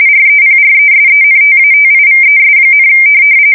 RTTY, about 45.45 baud. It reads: REPRESENTATIVES,
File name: fsk170 / USS-Missouri